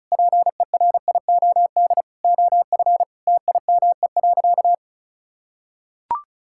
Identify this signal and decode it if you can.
Morse code, heard as a keyed tone, about 34 wpm, 700 Hz. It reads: PERIOD OF TIME.
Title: Sentences from Top 1000 Words - Review Part 5 - 35wpm